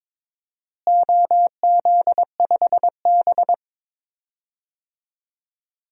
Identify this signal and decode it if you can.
Morse code, heard as a keyed tone, about 22 wpm, 700 Hz. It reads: OZ5B
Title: DXCC-100 Call Signs - 4 Characters - Repeated 3x - 22wpm